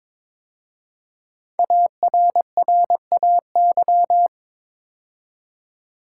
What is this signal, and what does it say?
Morse code, heard as a keyed tone, about 22 wpm, 690 Hz. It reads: ARRAY
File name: 5 Letter Words - First 3 Letters Given - 22wpm